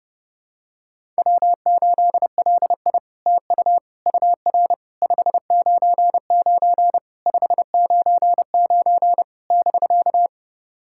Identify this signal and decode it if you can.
Morse code, heard as a keyed tone, about 30 wpm, 700 Hz. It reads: W8LI TU UR 599 599 <BK>
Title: SOTA QSO Practice 30wpm